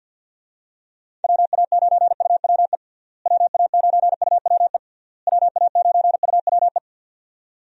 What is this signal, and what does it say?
Morse code, heard as a keyed tone, about 52 wpm, 700 Hz. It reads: WA9UWE WA9UWE WA9UWE
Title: USA Call Signs - 6 Characters - Repeated 3x - 50wpm